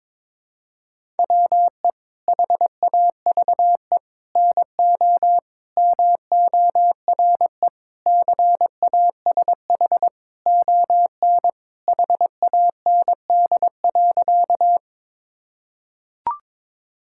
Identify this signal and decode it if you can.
Morse code, heard as a keyed tone, about 22 wpm, 700 Hz. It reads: WE HAVE NO MORE CASH ON HAND.
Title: Sentences from Top 1000 Words - Review Part 5 - 22wpm